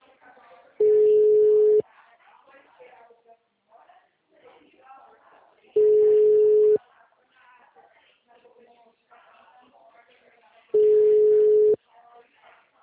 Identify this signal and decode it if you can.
Ringback tone (Europe, 425 Hz), with speech.